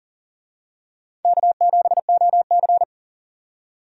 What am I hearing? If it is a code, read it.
Morse code, heard as a keyed tone, about 40 wpm, 700 Hz. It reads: K7OC
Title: USA Call Signs - 4 Characters - 40wpm